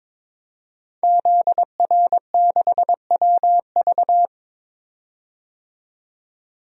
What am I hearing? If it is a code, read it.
Morse code, heard as a keyed tone, about 22 wpm, 700 Hz. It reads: ZR6WV
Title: DXCC-100 Call Signs - 5 Characters - 22wpm